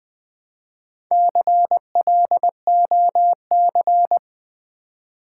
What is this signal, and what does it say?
Morse code, heard as a keyed tone, about 21 wpm, 700 Hz. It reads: CLOC
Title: Build A Word - 1 Letter At A Time - 5-Letter Words - 20wpm